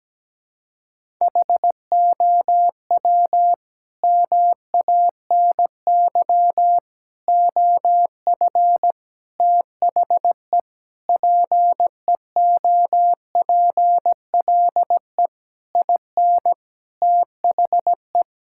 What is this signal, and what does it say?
Morse code, heard as a keyed tone, about 17 wpm, 700 Hz. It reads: HOW MANY OF THE PEOPLE IN THE
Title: Sentences from Top 1000 Words - Review Part 6 - 17wpm